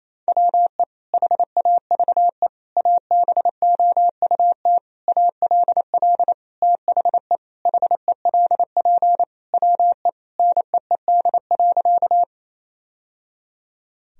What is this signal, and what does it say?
Morse code, heard as a keyed tone, about 28 wpm, 700 Hz. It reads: WE HAVE ABOUT ALL THE HELP WE NEED.
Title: Sentences from Top 1000 Words - Review Part 1 - 28wpm